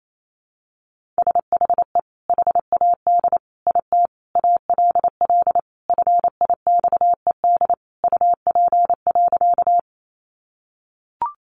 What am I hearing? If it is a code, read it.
Morse code, heard as a keyed tone, about 28 wpm, 700 Hz. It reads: SHE HAD IT ALL FIXED UP.
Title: Sentences from Top 1100 Words 28wpm